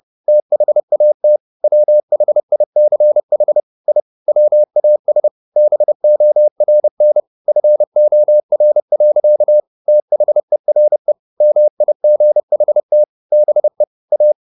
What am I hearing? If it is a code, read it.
Morse code, heard as a keyed tone, about 30 wpm, 600 Hz. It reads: THAT WHICH I WAS BORN FOR. THERE MIGHT BE A
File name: the old man and the sea / the old man and the sea 1 30wpm